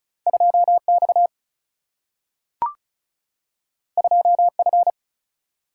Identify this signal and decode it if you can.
Morse code, heard as a keyed tone, about 34 wpm, 700 Hz. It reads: 2X 2F
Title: Single-Letter Single-Number 35wpm